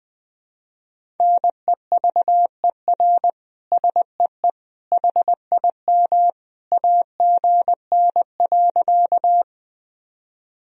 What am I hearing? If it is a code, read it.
Morse code, heard as a keyed tone, about 20 wpm, 700 Hz. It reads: NEVER SEE HIM AGN.
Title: Sentences from Top 300 Words and Abbreviations 20wpm